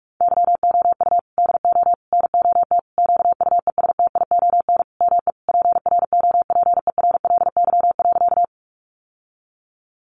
Morse code, heard as a keyed tone, about 44 wpm, 700 Hz. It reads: YOU DO NOT QUESTION ME PROPERLY.
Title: Sentences from Top 1100 Words 45wpm